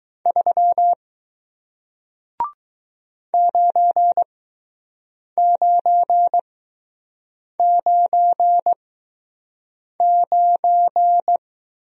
Morse code, heard as a keyed tone, about 20 wpm, spaced as for 6 wpm, 700 Hz. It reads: 3 9999